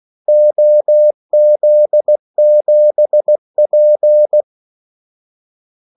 Morse code, heard as a keyed tone, about 16 wpm, 600 Hz. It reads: OZ7P
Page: Kaldesignaler 16 wpm